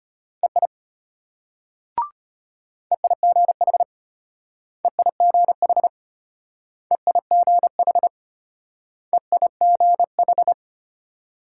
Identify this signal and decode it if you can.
Morse code, heard as a keyed tone, about 33 wpm, 700 Hz. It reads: EI EIGH EIGH EIGH EIGH